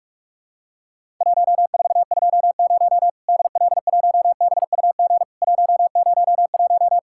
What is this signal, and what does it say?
Morse code, heard as a keyed tone, about 44 wpm, 700 Hz. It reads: 1420 DL1BUG 101
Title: Worked All Europe DX Contest - QTC Exchange - 45wpm